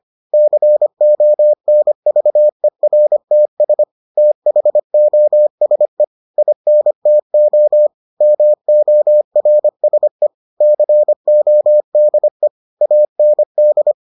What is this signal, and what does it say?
Morse code, heard as a keyed tone, about 25 wpm, 600 Hz. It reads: CONVERTS THOSE INTO MORSE CODE AND